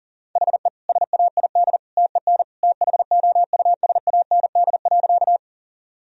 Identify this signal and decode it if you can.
Morse code, heard as a keyed tone, about 40 wpm, 700 Hz. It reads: HE SAID TEN THOUSAND.
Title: Sentences from Top 1000 Words - Review Part 3 - 40wpm